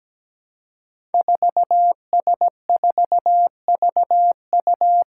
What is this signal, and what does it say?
Morse code, heard as a keyed tone, about 17 wpm, 700 Hz. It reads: 4S4VU
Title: Five Challenging Letter-Number Combinations 17wpm